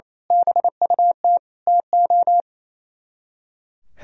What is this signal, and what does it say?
Morse code, heard as a keyed tone, about 27 wpm, 700 Hz. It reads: BUT TO
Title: Speed-Racing - Sets of 5 Words 28wpm